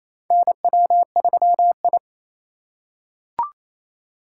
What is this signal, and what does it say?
Morse code, heard as a keyed tone, about 28 wpm, 700 Hz. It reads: NW3I
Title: USA Call Signs - 4x Character Spacing - 28wpm